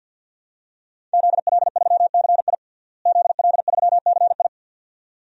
Morse code, heard as a keyed tone, about 51 wpm, 700 Hz. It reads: ZL3CI ZL3CI
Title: DXCC-100 Call Signs - 5 Characters - Repeated 2x - 50wpm